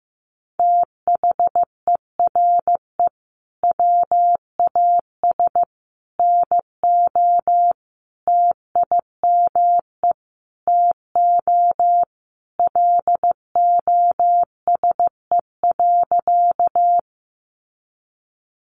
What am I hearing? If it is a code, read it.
Morse code, heard as a keyed tone, about 15 wpm, 700 Hz. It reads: THERE WAS NO TIME TO LOSE.
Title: Sentences from Top 1100 Words 15wpm